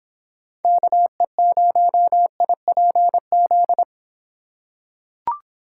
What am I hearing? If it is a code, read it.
Morse code, heard as a keyed tone, about 26 wpm, 700 Hz. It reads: KE0IPZ